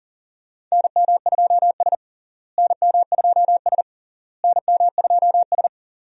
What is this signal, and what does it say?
Morse code, heard as a keyed tone, about 40 wpm, 700 Hz. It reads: NM2S NM2S NM2S
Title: USA Call Signs - 4 Characters - Repeated 3x - 40wpm